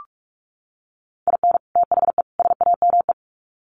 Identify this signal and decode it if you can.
Morse code, heard as a keyed tone, about 44 wpm, 700 Hz. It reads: IN THE SAME
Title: Sets of 3 Words 45wpm